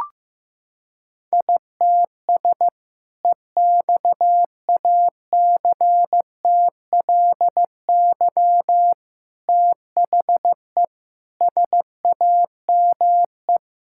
Morse code, heard as a keyed tone, about 15 wpm, 700 Hz. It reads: ITS EXACTLY THE SAME
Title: Sentences from Top 1000 Words - Review Part 6 - 15wpm